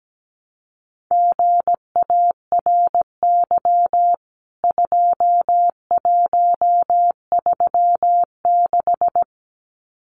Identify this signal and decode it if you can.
Morse code, heard as a keyed tone, about 17 wpm, 700 Hz. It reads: GARY 2136
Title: CWT Exchange 17wpm